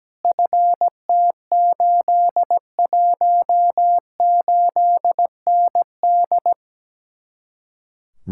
Morse code, heard as a keyed tone, about 17 wpm, 700 Hz. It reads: FT818ND
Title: Speed-Racing Common QSO Phrases - 17wpm